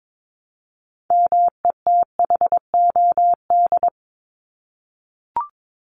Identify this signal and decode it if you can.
Morse code, heard as a keyed tone, about 22 wpm, 700 Hz. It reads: METHOD
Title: Top 501-750 Words 22wpm